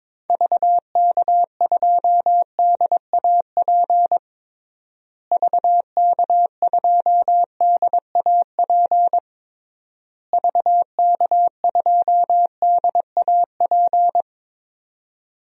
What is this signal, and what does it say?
Morse code, heard as a keyed tone, about 22 wpm, 700 Hz. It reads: VK2DAP VK2DAP VK2DAP
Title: DXCC-100 Call Signs - 6 Characters - Repeated 3x - 22wpm